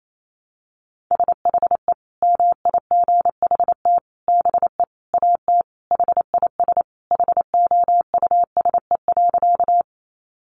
Morse code, heard as a keyed tone, about 28 wpm, 700 Hz. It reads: SHE MIGHT BE AT HIS HOUSE.
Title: Sentences from Top 300 Words 28wpm